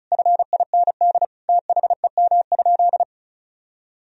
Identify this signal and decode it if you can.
Morse code, heard as a keyed tone, about 34 wpm, 700 Hz. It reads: FIND THEM?
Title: Sentences from Top 200 Words and Abbreviations 35wpm